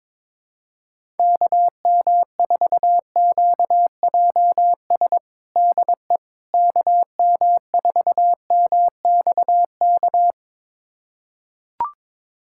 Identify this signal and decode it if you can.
Morse code, heard as a keyed tone, about 22 wpm, 700 Hz. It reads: KM4QJS DE KM4MXK
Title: USA Callsign DE Callsign - 6 Characters - 22wpm